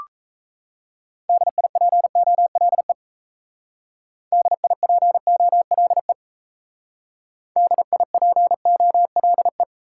Morse code, heard as a keyed tone, about 37 wpm, 700 Hz. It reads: DIPOLE DIPOLE DIPOLE